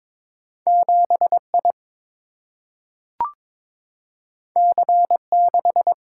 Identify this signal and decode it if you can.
Morse code, heard as a keyed tone, about 22 wpm, 700 Hz. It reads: 7 I C 6
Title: Single-Letter Single-Number 22wpm